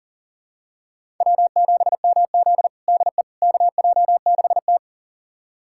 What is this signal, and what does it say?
Morse code, heard as a keyed tone, about 40 wpm, 700 Hz. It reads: W7MZ DE KJ6T
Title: USA Callsign DE Callsign - 4 Characters - 40wpm